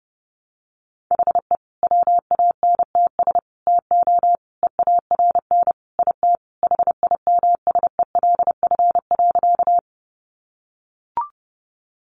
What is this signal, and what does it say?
Morse code, heard as a keyed tone, about 30 wpm, 700 Hz. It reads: HE WANTS TO EARN IT HIMSELF.
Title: Additional Easy Phrases 30wpm